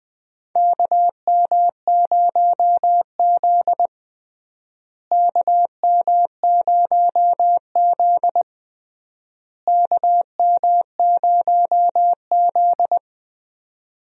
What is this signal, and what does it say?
Morse code, heard as a keyed tone, about 21 wpm, 700 Hz. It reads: KM0Z KM0Z KM0Z
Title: USA Call Signs - 4 Characters - Repeated 3x - 20wpm